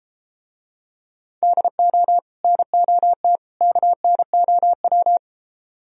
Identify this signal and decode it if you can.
Morse code, heard as a keyed tone, about 33 wpm, 700 Hz. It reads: DO NOT KNOW